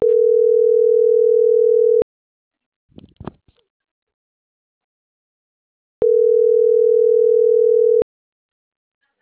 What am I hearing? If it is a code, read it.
Ringback tone (North America), 440 + 480 Hz.